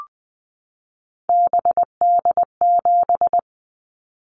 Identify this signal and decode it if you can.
Morse code, heard as a keyed tone, about 20 wpm, 700 Hz. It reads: BD7
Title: Callsign Prefixes for Top 100 DXCC - 20wpm